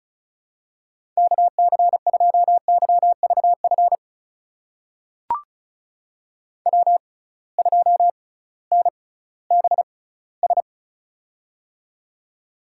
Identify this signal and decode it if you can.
Morse code, heard as a keyed tone, about 36 wpm, spaced as for 19 wpm, 700 Hz. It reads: KC2YVF W2NBS
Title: USA Call Signs - 6x Character Spacing - 35wpm